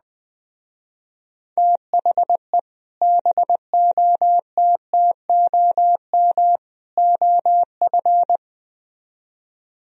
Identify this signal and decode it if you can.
Morse code, heard as a keyed tone, about 20 wpm, 700 Hz. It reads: THE BOTTOM OF
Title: Speed-Racing - Sets of 3 Words 17wpm